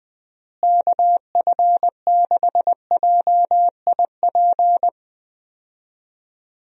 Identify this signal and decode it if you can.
Morse code, heard as a keyed tone, about 20 wpm, 700 Hz. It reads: KF6JIP
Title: USA Call Signs - 6 Characters - Repeated 3x - 20wpm